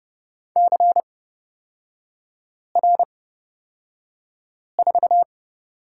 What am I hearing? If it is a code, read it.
Morse code, heard as a keyed tone, about 30 wpm, spaced as for 5 wpm, 700 Hz. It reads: CR4
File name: Lesson 38e - Letter q - Three Characters 30wpm